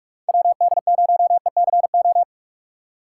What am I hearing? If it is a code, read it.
Morse code, heard as a keyed tone, about 46 wpm, 700 Hz. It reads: WD0ECO
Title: K1USN SST QSO Practice - 45wpm